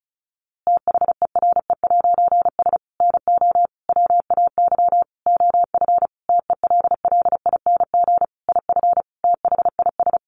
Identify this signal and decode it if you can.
Morse code, heard as a keyed tone, about 34 wpm, 700 Hz. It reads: THERE'S NO WAY OF TELLING IF THIS
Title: A Sound Of Thunder 35wpm